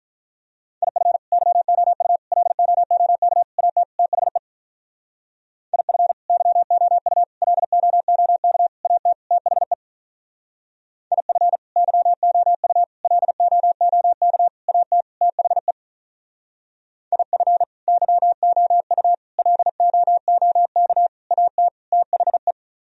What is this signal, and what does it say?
Morse code, heard as a keyed tone, about 44 wpm, 700 Hz. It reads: IF YOU LOOK AT THE IF YOU LOOK AT THE IF YOU LOOK AT THE IF YOU LOOK AT THE
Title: Speed-Racing - Sets of 5 Words 35wpm